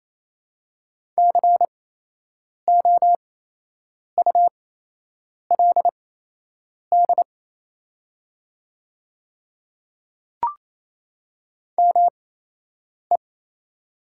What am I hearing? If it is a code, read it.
Morse code, heard as a keyed tone, about 28 wpm, spaced as for 8 wpm, 700 Hz. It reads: COULD ME
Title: 8x Character Spacing - Top 200 Words - 28wpm